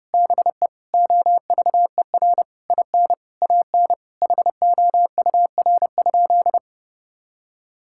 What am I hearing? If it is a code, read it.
Morse code, heard as a keyed tone, about 32 wpm, 700 Hz. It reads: BE OVER IN AN HOUR?
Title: Sentences from Top 1000 Words - Review Part 5 - 30wpm